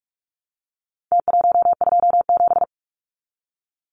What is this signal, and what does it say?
Morse code, heard as a keyed tone, about 44 wpm, 700 Hz. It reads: T127
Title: Numbers - Contest Style - 4-Digit 45wpm